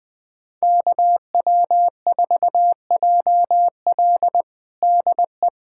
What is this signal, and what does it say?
Morse code, heard as a keyed tone, about 21 wpm, 700 Hz. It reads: KW4JL DE
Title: USA Callsign DE Callsign - 5 Characters - 20wpm